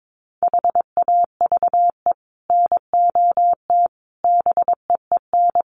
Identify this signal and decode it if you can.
Morse code, heard as a keyed tone, about 22 wpm, 700 Hz. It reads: HAVE NOT BEEN
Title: Sets of 5 Words Encore 22wpm